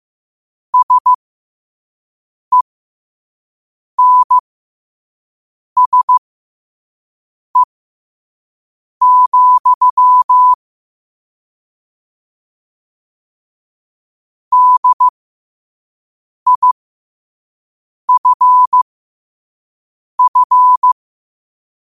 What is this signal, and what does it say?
Morse code, heard as a keyed tone, about 15 wpm, spaced as for 5 wpm, 1000 Hz. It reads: SENSE, DIFF
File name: qotdfiles / 5wpm